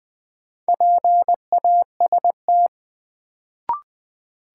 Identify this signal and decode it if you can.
Morse code, heard as a keyed tone, about 20 wpm, 700 Hz. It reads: PAST
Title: Sentences from Top 600 Words - New Word Drill - 20wpm